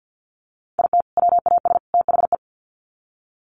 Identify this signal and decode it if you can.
Morse code, heard as a keyed tone, about 48 wpm, 700 Hz. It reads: IT WAS THE
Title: Sets of 3 Words 50wpm